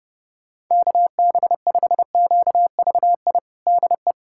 Morse code, heard as a keyed tone, about 30 wpm, 700 Hz. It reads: KB5QVI DE
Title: USA Callsign DE Callsign - 6 Characters - 30wpm